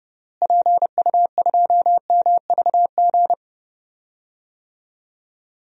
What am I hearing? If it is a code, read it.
Morse code, heard as a keyed tone, about 30 wpm, 700 Hz. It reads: PU2MVG
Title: DXCC-100 Call Signs - 6 Characters - Repeated 2x - 30wpm